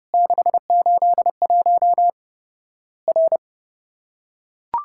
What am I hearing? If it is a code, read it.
Morse code, heard as a keyed tone, about 32 wpm, 700 Hz. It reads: 681 R
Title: Worked All Europe DX Contest - QTC Exchange - 30wpm